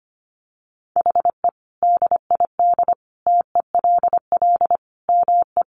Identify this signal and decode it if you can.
Morse code, heard as a keyed tone, about 25 wpm, 700 Hz. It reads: HE DID TELL ME
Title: Sentences from Top 1000 Words 25wpm